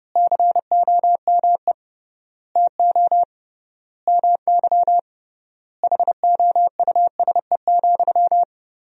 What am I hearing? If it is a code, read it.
Morse code, heard as a keyed tone, about 30 wpm, 700 Hz. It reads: COME TO MY HOUSE,
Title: 3x Word-Spacing - Sentences from Top 400 Words - 30wpm